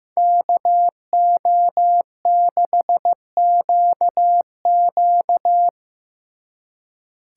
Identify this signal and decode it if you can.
Morse code, heard as a keyed tone, about 18 wpm, 700 Hz. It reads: KO6QQ
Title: Speed Racing - USA 2x2 Call Signs 15wpm